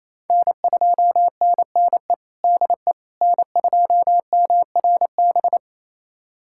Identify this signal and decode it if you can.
Morse code, heard as a keyed tone, about 28 wpm, 700 Hz. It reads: N2NNE DE N2MRB
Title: USA Callsign DE Callsign - 5 Characters - 28wpm